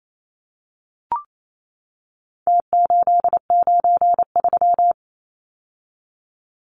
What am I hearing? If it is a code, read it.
Morse code, heard as a keyed tone, about 28 wpm, 700 Hz. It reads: T893
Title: Numbers - Contest Style - 4-Digit 28wpm